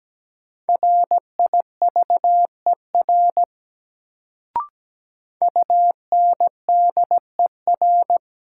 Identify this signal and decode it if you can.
Morse code, heard as a keyed tone, about 17 wpm, 700 Hz. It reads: RIVER UNDER
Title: Sentences from Top 300 Words - New Word Drill - 17wpm